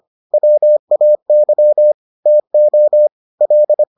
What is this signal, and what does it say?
Morse code, heard as a keyed tone, about 25 wpm, 600 Hz. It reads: WAY TO L